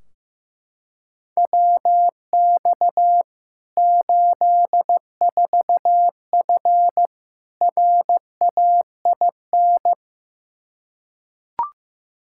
Morse code, heard as a keyed tone, about 15 wpm, 700 Hz. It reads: WX 84F RAIN
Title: K1USN SST QSO Practice 15wpm